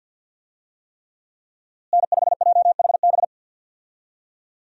Morse code, heard as a keyed tone, about 50 wpm, 700 Hz. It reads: N5JHB
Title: USA Call Signs - 5 Characters - Repeated 3x - 50wpm